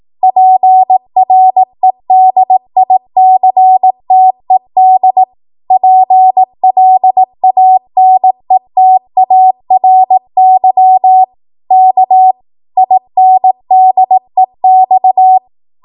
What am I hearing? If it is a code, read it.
Morse code, heard as a keyed tone, about 18 wpm, 750 Hz. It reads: PREDICTED PLANETARY K INDEX
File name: Morse / 18WPM